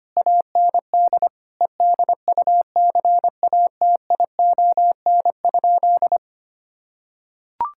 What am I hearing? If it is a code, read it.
Morse code, heard as a keyed tone, about 25 wpm, 700 Hz. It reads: AND EDUCATION?
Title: Sentences from Top 1000 Words - Review Part 6 - 25wpm